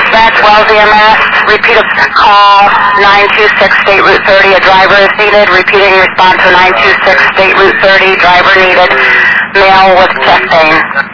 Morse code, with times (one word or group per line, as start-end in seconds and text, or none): none